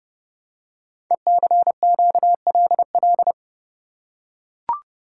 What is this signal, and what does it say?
Morse code, heard as a keyed tone, about 28 wpm, 700 Hz. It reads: ECQLL
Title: Code Groups - 5 Letters - 30wpm